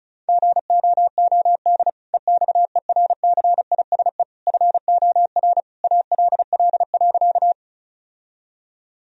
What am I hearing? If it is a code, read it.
Morse code, heard as a keyed tone, about 36 wpm, 700 Hz. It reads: GOOD EXERCISE FOR ALL.
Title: Sentences from Top 1000 Words - Review Part 4 - 35wpm